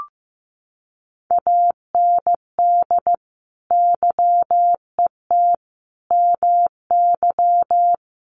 Morse code, heard as a keyed tone, about 15 wpm, 700 Hz. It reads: AND YET MY
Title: Sentences from Top 1200 Words 15wpm